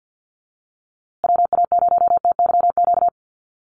Morse code, heard as a keyed tone, about 48 wpm, 700 Hz. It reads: WA0TYQ
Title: USA Call Signs Repeated 2x 50wpm